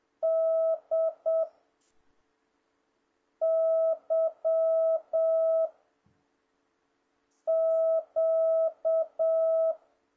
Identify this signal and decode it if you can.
Morse code, heard as a keyed tone, about 7 wpm, 650 Hz. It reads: D Y Q
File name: jh8cbh / 2026cw